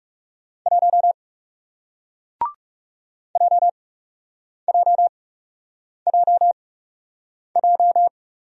Morse code, heard as a keyed tone, about 39 wpm, spaced as for 8 wpm, 700 Hz. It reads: 1 JJJJ